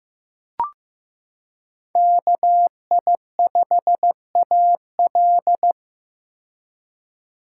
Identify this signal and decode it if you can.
Morse code, heard as a keyed tone, about 15 wpm, 700 Hz. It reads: KI5AL
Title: K1USN SST QSO Practice - 15wpm